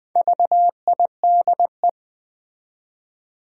Morse code, heard as a keyed tone, about 20 wpm, 700 Hz. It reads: VIDE
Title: Build A Word - 1 Letter At A Time - 3 to 5-Letter Words - 20wpm